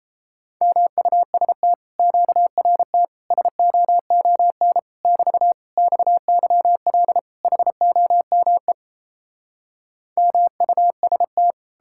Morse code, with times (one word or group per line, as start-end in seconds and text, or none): 0.6-1.7 MUST
2.0-3.0 QRT
3.3-4.8 SOON
5.0-5.5 <BT>
5.8-7.2 XYL
7.4-8.7 HOME
10.2-11.5 MUST